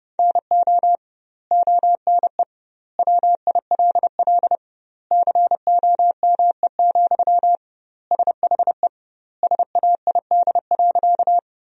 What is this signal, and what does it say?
Morse code, heard as a keyed tone, about 32 wpm, 700 Hz. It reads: NO ONE WILL COME, SHE SAID.
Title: 2x Word-Spacing - Sentences from Top 100 Words - 30wpm